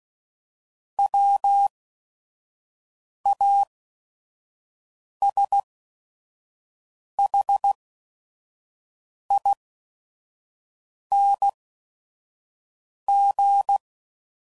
Morse code, heard as a keyed tone, about 16 wpm, spaced as for 5 wpm, 800 Hz. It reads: WASHING
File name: Suffixes / Special ING's / VerySlowToFaster ING's